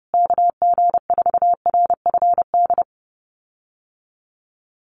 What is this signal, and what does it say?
Morse code, heard as a keyed tone, about 30 wpm, 700 Hz. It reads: KG4RFD
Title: USA Call Signs Repeated 3x 30wpm